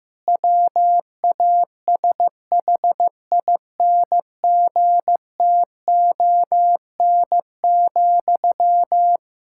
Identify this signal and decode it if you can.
Morse code, heard as a keyed tone, about 15 wpm, 700 Hz. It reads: WASHINGTON,
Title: K1USN SST QSO Practice 15wpm